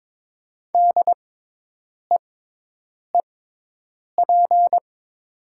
Morse code, heard as a keyed tone, about 22 wpm, spaced as for 8 wpm, 700 Hz. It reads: DEEP
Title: 6x Character Spacing - Top 500 Words - 22wpm